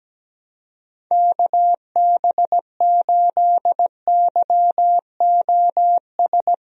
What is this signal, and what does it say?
Morse code, heard as a keyed tone, about 17 wpm, 700 Hz. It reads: KB8YOS
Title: USA Call Signs - 6 Characters - Repeated 2x - 17wpm